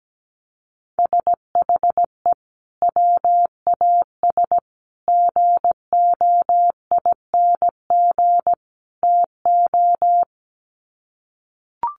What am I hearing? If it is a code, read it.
Morse code, heard as a keyed tone, about 17 wpm, 700 Hz. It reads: SHE WAS GOING TO
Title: Sets of 4 Words 17wpm